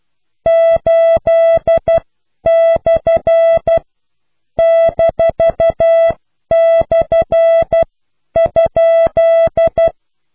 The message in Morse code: 8/-/?